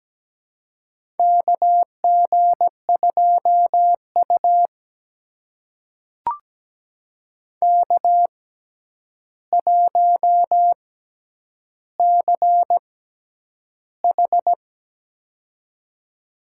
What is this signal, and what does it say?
Morse code, heard as a keyed tone, about 17 wpm, spaced as for 9 wpm, 700 Hz. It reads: KG2U K1CH
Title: USA Call Signs - 4 Characters - 6x Character Spacing - 17wpm